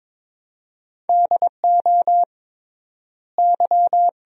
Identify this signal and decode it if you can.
Morse code, heard as a keyed tone, about 22 wpm, 700 Hz. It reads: DO Y
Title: 3x Word-Spacing - Sentences from Top 300 Words - 22wpm